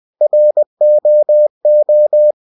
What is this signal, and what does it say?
Morse code, heard as a keyed tone, about 20 wpm, 600 Hz. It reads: ROO
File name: cw-books / winnie-the-pooh / winnie-20wpm-page-0087